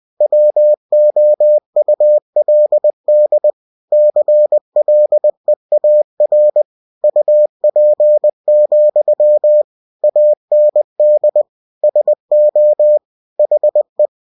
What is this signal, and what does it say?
Morse code, heard as a keyed tone, about 20 wpm, 600 Hz. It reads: WOULD CLEAR UP, AND SO HE
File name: cw-books / winnie-the-pooh / winnie-20wpm-page-0035